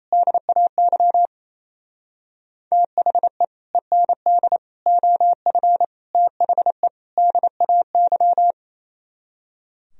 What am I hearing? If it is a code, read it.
Morse code, heard as a keyed tone, about 37 wpm, 700 Hz. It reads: DAY THE END OF THE DAY